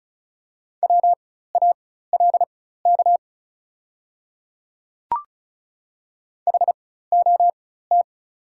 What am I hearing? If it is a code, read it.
Morse code, heard as a keyed tone, about 36 wpm, spaced as for 16 wpm, 690 Hz. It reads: WALK HOT
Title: 4x Character Spacing - Top 300 Words - 35wpm